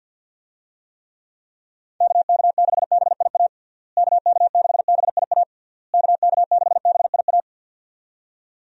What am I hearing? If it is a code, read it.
Morse code, heard as a keyed tone, about 51 wpm, 700 Hz. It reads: KK6BIA KK6BIA KK6BIA
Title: USA Call Signs - 6 Characters - Repeated 3x - 50wpm